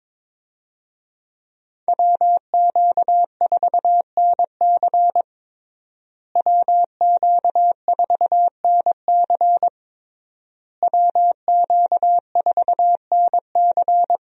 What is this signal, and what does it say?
Morse code, heard as a keyed tone, about 22 wpm, 700 Hz. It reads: WQ4NC WQ4NC WQ4NC
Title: USA Call Signs - 5 Characters - Repeated 3x - 22wpm